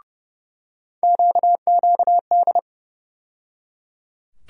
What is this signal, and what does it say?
Morse code, heard as a keyed tone, about 30 wpm, 700 Hz. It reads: QQD
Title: USA Call Sign Postfixes - 3 Letters 30wpm